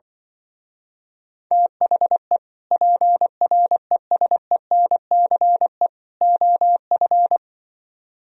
Morse code, heard as a keyed tone, about 25 wpm, 700 Hz. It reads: THE PRESENCE OF
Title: Speed-Racing - Sets of 3 Words 20wpm